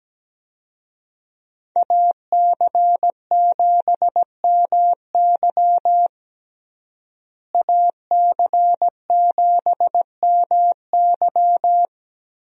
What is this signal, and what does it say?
Morse code, heard as a keyed tone, about 17 wpm, 700 Hz. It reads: AC7MY AC7MY
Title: USA Call Signs - 5 Characters - Repeated 2x - 17wpm